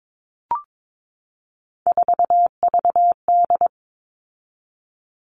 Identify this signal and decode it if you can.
Morse code, heard as a keyed tone, about 22 wpm, 700 Hz. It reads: 4VD
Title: All Cut Numbers - 3-Digit 22wpm